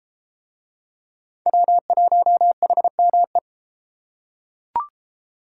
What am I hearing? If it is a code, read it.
Morse code, heard as a keyed tone, about 30 wpm, 700 Hz. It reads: W1HME